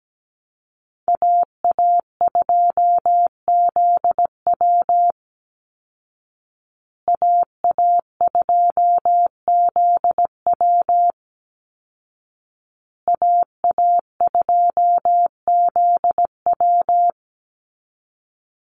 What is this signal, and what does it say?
Morse code, heard as a keyed tone, about 17 wpm, 700 Hz. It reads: AA2ZW AA2ZW AA2ZW
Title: Call Signs Repeated 2x - 17wpm